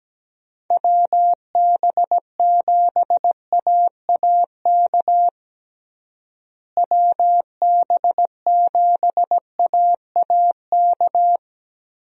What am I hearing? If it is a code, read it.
Morse code, heard as a keyed tone, about 17 wpm, 700 Hz. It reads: WB7AAK WB7AAK
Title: USA Call Signs - 6 Characters - Repeated 2x - 17wpm